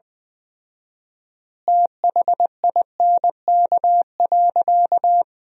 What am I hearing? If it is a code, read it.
Morse code, heard as a keyed tone, about 20 wpm, 700 Hz. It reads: THINK.
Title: 4x Word-Spacing - Sentences from Top 500 Words - 20wpm